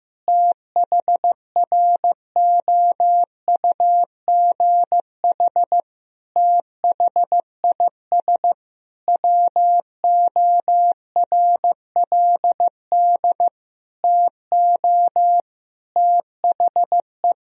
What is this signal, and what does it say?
Morse code, heard as a keyed tone, about 15 wpm, 700 Hz. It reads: THROUGH THIS WORLD TO THE
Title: Sentences from Top 1000 Words - Review Part 2 - 15wpm